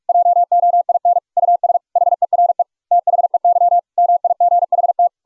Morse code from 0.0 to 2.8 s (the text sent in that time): JOIN US HERE